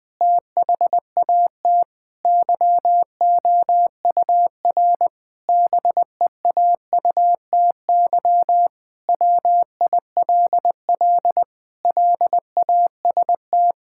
Morse code, heard as a keyed tone, about 21 wpm, 700 Hz. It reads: THAT YOUR BEAUTY WILL LAST
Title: Sentences from Top 1600 Words 20wpm